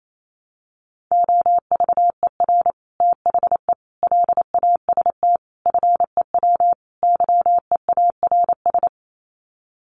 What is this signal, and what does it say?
Morse code, heard as a keyed tone, about 28 wpm, 700 Hz. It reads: OVER THE LAST FEW YEARS
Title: Sets of 5 Words Encore 28wpm